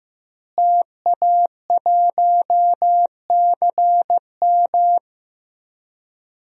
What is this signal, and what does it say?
Morse code, heard as a keyed tone, about 15 wpm, 700 Hz. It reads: TA1CM
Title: DXCC-100 Call Signs - 5 Characters - Repeated 2x - 15wpm